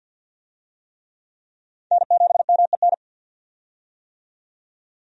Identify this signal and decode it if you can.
Morse code, heard as a keyed tone, about 51 wpm, 700 Hz. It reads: N7MEN
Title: USA Call Signs - 5 Characters - 50wpm